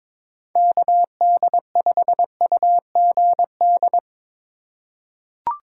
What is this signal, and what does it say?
Morse code, heard as a keyed tone, about 22 wpm, 700 Hz. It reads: KD5UGD
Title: USA Call Signs - 6 Characters - 4x Character Spacing - 22wpm